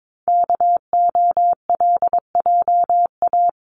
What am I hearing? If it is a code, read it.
Morse code, heard as a keyed tone, about 22 wpm, 700 Hz. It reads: KOLJA
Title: CWT Exchange 22wpm